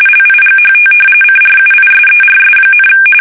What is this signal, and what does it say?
RTTY, about 45.45 baud. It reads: LIGHT CRUISERS,
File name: fsk850 / battleshipnj